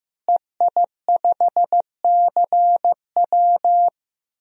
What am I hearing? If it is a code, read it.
Morse code, heard as a keyed tone, about 15 wpm, 700 Hz. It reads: EI5CW
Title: DXCC-100 Call Signs - 5 Characters - Repeated 2x - 15wpm